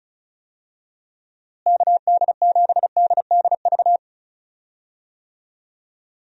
Morse code, heard as a keyed tone, about 34 wpm, 700 Hz. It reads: KD7DDV
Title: USA Call Signs - 6 Characters - Repeated 2x - 35wpm